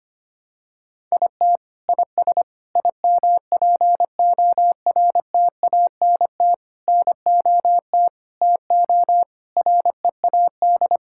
Morse code, heard as a keyed tone, about 25 wpm, 700 Hz. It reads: IT IS IMPORTANT NOT TO READ
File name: Sets of 9 Words - 25wpm